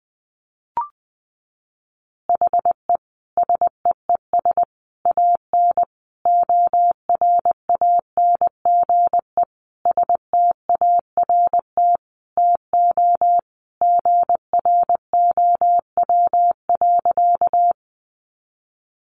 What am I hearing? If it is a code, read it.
Morse code, heard as a keyed tone, about 20 wpm, 700 Hz. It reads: HE SEES AN ORANGE START TO GROW.
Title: Easy Phrases 20wpm